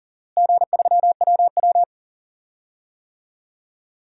Morse code, heard as a keyed tone, about 37 wpm, 700 Hz. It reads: G3WW
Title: DXCC-100 Call Signs - 4 Characters - Repeated 3x - 40wpm